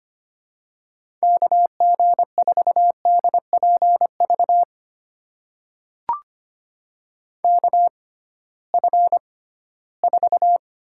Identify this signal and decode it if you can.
Morse code, heard as a keyed tone, about 25 wpm, spaced as for 16 wpm, 700 Hz. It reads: KG4DPV KF4
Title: USA Call Signs - 6 Characters - 6x Character Spacing - 25wpm